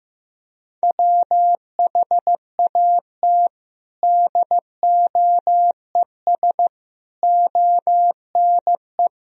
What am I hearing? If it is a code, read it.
Morse code, heard as a keyed tone, about 15 wpm, 700 Hz. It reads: WHAT DOES ONE
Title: Sentences from Top 200 Words and Abbreviations 15wpm